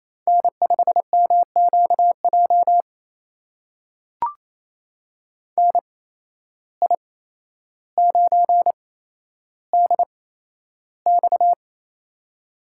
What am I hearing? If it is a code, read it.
Morse code, heard as a keyed tone, about 28 wpm, spaced as for 12 wpm, 700 Hz. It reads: N5MQJ NI9DX
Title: USA Call Signs - 5 Characters - 8x Character Spacing - 28wpm